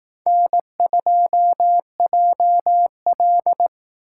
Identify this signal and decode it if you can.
Morse code, heard as a keyed tone, about 18 wpm, 700 Hz. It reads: N2JL
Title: Speed Racing - USA 1x2 Call Signs 15wpm